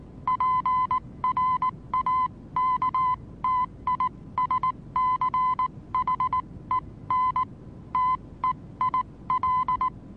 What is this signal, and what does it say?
Morse code, heard as a keyed tone, about 19 wpm, spaced as for 16 wpm, 1020 Hz. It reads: PRAKTISCHEN TEIL
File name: repos / Morsekurs / DJ4UF Morsen26